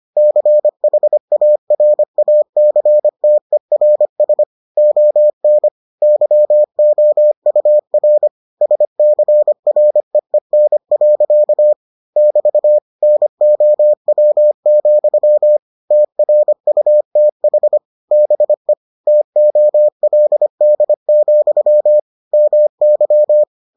Morse code, heard as a keyed tone, about 25 wpm, 600 Hz. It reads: CHARACTERS ON YOUR SCREEN. <BT> NOW, TRUTH BE TOLD, MY